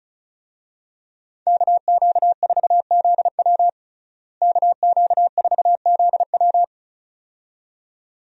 Morse code, heard as a keyed tone, about 34 wpm, 700 Hz. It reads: KQ4ZW KQ4ZW
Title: USA Call Signs - 5 Characters - Repeated 2x - 35wpm